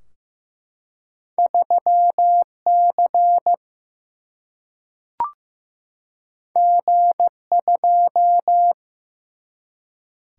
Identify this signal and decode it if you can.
Morse code, heard as a keyed tone, about 15 wpm, 700 Hz. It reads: 3C G2
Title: Single-Letter Single-Number 15wpm